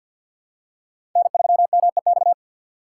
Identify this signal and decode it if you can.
Morse code, heard as a keyed tone, about 47 wpm, 700 Hz. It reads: N3MEX
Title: USA Call Signs - 5 Characters - 4x Character Spacing - 50wpm